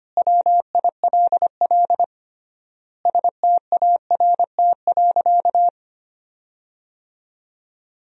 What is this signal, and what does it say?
Morse code, heard as a keyed tone, about 25 wpm, 700 Hz. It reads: WILL START.
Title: 3x Word-Spacing - Sentences from Top 300 Words - 25wpm